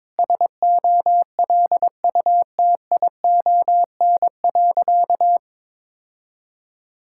Morse code, heard as a keyed tone, about 22 wpm, 700 Hz. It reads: SOLUTION.
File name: practice / 22wpm / Sentences with 4-Word Phrases - 22wpm